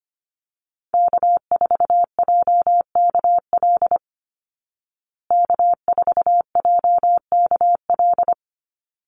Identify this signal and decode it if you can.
Morse code, heard as a keyed tone, about 25 wpm, 700 Hz. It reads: K4JKL K4JKL
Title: Call Signs Repeated 3x - 25wpm